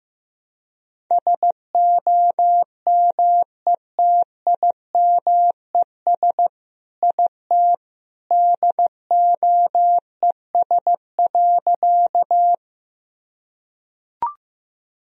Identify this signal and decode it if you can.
Morse code, heard as a keyed tone, about 15 wpm, 700 Hz. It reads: SOMETIMES IT DOES.
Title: Sentences from Top 1000 Words - Review Part 3 - 15wpm